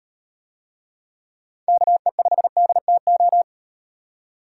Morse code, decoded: KE5DTO

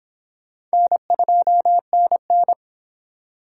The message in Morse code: N2NN

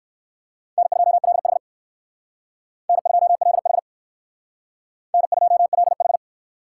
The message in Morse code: N2LH N2LH N2LH

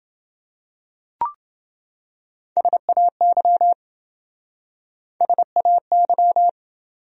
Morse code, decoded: SAY SAY

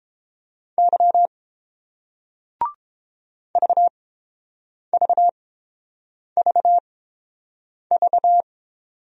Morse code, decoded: Y VVVV

